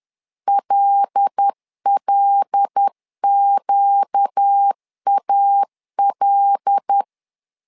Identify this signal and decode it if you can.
Morse code, heard as a keyed tone, about 10 wpm, 800 Hz. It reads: LLQAL
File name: LEZIONI / 5 LBVC / 5 L 10wpm 2259